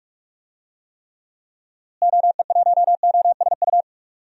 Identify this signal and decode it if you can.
Morse code, heard as a keyed tone, about 46 wpm, 700 Hz. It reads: OE1OSU
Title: DXCC-100 Call Signs - 6 Characters - Repeated 2x - 45wpm